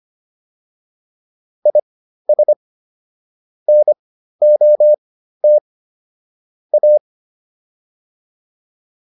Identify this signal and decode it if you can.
Morse code, heard as a keyed tone, about 25 wpm, spaced as for 13 wpm, 600 Hz. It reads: IS NOT A